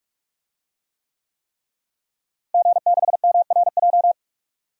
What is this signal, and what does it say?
Morse code, heard as a keyed tone, about 46 wpm, 700 Hz. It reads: G6MRJ
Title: DXCC-100 Call Signs - 5 Characters - Repeated 3x - 45wpm